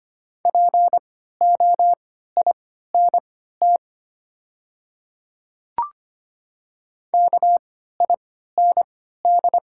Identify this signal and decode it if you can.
Morse code, heard as a keyed tone, about 25 wpm, spaced as for 14 wpm, 690 Hz. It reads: POINT KIND
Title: 3x Character Spacing - Top 300 Words - 25wpm